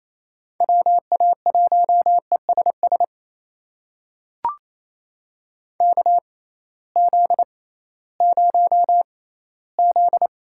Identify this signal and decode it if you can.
Morse code, heard as a keyed tone, about 27 wpm, spaced as for 16 wpm, 700 Hz. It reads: WA1ESS KZ0Z
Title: USA Call Signs - 6x Character Spacing - 28wpm